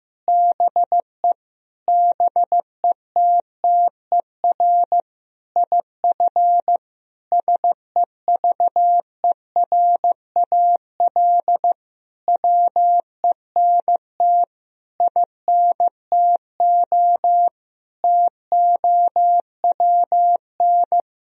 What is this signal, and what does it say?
Morse code, heard as a keyed tone, about 15 wpm, 700 Hz. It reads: BE BETTER IF SEVERAL WENT INTO TOWN
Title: Sentences from Top 1000 Words - Review Part 2 - 15wpm